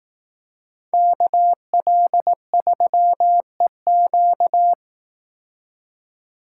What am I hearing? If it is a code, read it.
Morse code, heard as a keyed tone, about 18 wpm, 700 Hz. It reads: KL3EQ
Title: Speed Racing - USA 2x2 Call Signs 15wpm